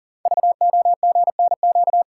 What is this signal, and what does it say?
Morse code, heard as a keyed tone, about 38 wpm, 700 Hz. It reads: VOGNQ
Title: Code Groups - 5 Letters - 40wpm